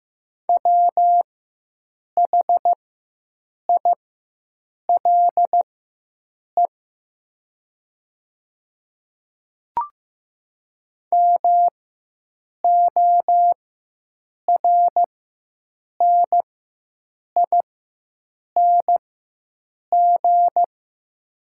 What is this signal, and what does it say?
Morse code, heard as a keyed tone, about 15 wpm, spaced as for 7 wpm, 700 Hz. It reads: WHILE MORNING
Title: 4x Character Spacing - Top 500 Words - 15wpm